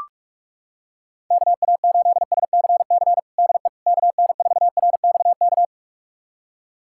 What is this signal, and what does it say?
Morse code, heard as a keyed tone, about 44 wpm, 700 Hz. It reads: KA8SCC DE KN4RXK
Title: USA Callsign DE Callsign 45wpm